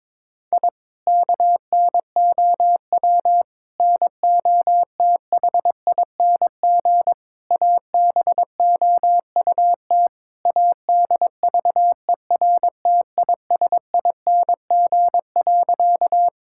Morse code, heard as a keyed tone, about 21 wpm, 700 Hz. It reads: I KNOW NOTHING ABOUT ADVERTISING.
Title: Sentences from Top 1000 Words - Review Part 6 - 22wpm